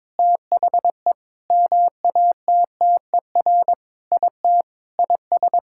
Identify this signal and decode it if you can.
Morse code, heard as a keyed tone, about 22 wpm, 700 Hz. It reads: THE MATTER IT IS
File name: Sentences with 7-Word Phrases - 22wpm